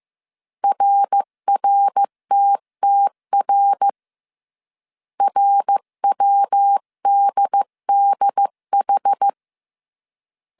Morse code, heard as a keyed tone, about 15 wpm, 800 Hz. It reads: RRTTR RWDDH
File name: CW / DET / 1 3 1500C S C 14wpm 2458